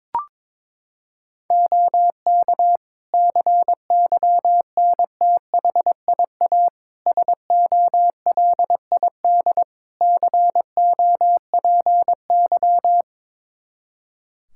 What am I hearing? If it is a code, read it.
Morse code, heard as a keyed tone, about 22 wpm, 700 Hz. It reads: OK CYNTHIA SOLID COPY
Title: Practice QSO - Exchange 3 - More Info - 22wpm